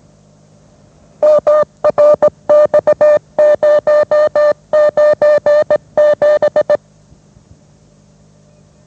Morse code, heard as a keyed tone, about 18 wpm, 600 Hz. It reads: MRX097
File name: mrx / mrx097 20wpm